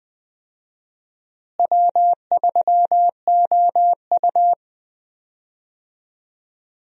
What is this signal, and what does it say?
Morse code, heard as a keyed tone, about 20 wpm, 700 Hz. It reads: W3OU
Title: USA Call Signs - 4 Characters - Repeated 2x - 20wpm